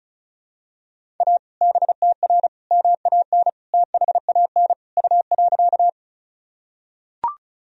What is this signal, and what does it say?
Morse code, heard as a keyed tone, about 34 wpm, 700 Hz. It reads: A BTR MAN THAN U.
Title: Sentences from Top 400 Words and Abbreviations 35wpm